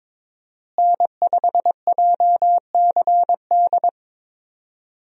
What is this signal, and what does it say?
Morse code, heard as a keyed tone, about 21 wpm, 700 Hz. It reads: N5JCD
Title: USA Call Signs - 5 Characters - Repeated 2x - 22wpm